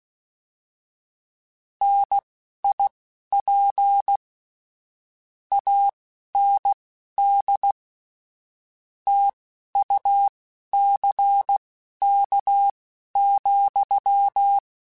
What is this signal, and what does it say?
Morse code, heard as a keyed tone, about 16 wpm, spaced as for 11 wpm, 800 Hz. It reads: NIP AND TUCK,
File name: Misc / Misc-mp3s / Word Pairs @10